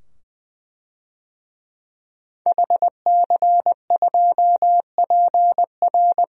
Morse code, heard as a keyed tone, about 20 wpm, 700 Hz. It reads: HC2PR
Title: DXCC-100 Call Signs - 5 Characters - 20wpm